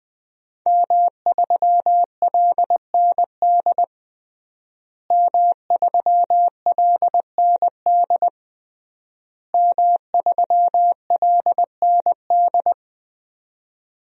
Morse code, heard as a keyed tone, about 20 wpm, 700 Hz. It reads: M3LND M3LND M3LND
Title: DXCC-100 Call Signs - 5 Characters - Repeated 3x - 20wpm